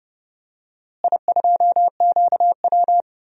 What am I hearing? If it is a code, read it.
Morse code, heard as a keyed tone, about 32 wpm, 700 Hz. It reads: I2QW
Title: DXCC-100 Call Signs - 4 Characters - Repeated 3x - 30wpm